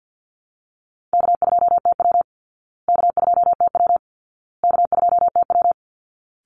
Morse code, heard as a keyed tone, about 48 wpm, 700 Hz. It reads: K2TW K2TW K2TW
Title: Easy Call Signs Repeated 3x - 50wpm